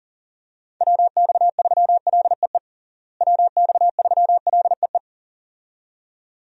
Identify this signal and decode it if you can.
Morse code, heard as a keyed tone, about 40 wpm, 700 Hz. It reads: WX3LEE WX3LEE
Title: USA Call Signs - 6 Characters - Repeated 2x - 40wpm